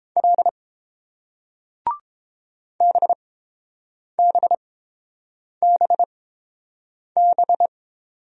Morse code, heard as a keyed tone, about 30 wpm, spaced as for 7 wpm, 700 Hz. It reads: L BBBB